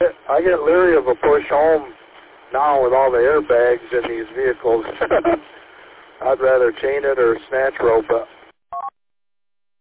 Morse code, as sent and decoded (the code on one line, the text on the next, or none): none